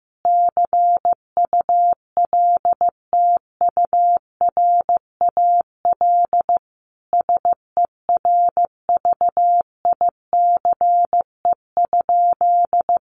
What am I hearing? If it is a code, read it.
Morse code, heard as a keyed tone, about 15 wpm, 700 Hz. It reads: CULTURAL SERVICE?
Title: Sentences from Top 1500 Words 15wpm